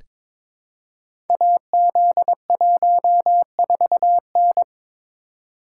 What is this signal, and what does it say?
Morse code, heard as a keyed tone, about 23 wpm, 700 Hz. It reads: AZ14N
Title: Code Groups - 5 Letter:Numbers - 22wpm